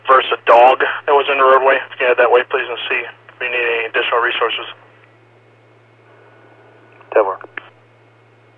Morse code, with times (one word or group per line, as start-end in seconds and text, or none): none